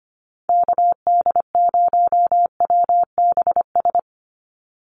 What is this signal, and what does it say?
Morse code, heard as a keyed tone, about 25 wpm, 700 Hz. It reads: KD0WBS
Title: USA Call Signs 25wpm